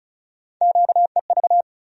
Morse code, heard as a keyed tone, about 34 wpm, 700 Hz. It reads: QEV
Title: USA Call Sign Postfixes - 3 Letters 35wpm